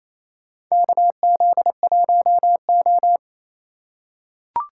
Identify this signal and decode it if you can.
Morse code, heard as a keyed tone, about 28 wpm, 700 Hz. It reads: KZ1O
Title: USA Call Signs - 4 Characters - 4x Character Spacing - 28wpm